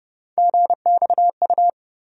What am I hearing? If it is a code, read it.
Morse code, heard as a keyed tone, about 32 wpm, 700 Hz. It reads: GXU
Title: USA Call Sign Postfixes - 3 Letters 30wpm